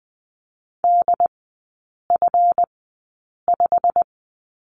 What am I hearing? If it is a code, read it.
Morse code, heard as a keyed tone, about 20 wpm, spaced as for 8 wpm, 700 Hz. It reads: DF5
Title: 138 Lesson 23e - Letter f - Three Characters 20wpm